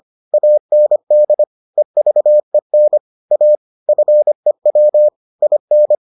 Morse code, heard as a keyed tone, about 25 wpm, 600 Hz. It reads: AND EVEN A FEW IN